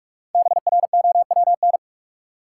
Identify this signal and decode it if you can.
Morse code, heard as a keyed tone, about 41 wpm, 700 Hz. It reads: BROWN
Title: Five Letter Words - 45wpm